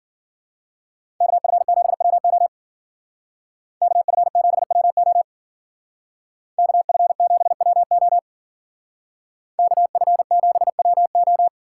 Morse code, heard as a keyed tone, about 50 wpm, 700 Hz. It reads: KF7WO KF7WO KF7WO KF7WO